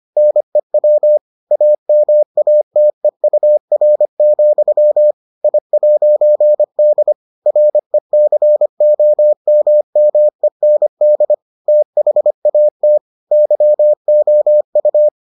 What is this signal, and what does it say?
Morse code, heard as a keyed tone, about 25 wpm, 600 Hz. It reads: NEW AMATEUR, I'D RECOMMEND THAT YOU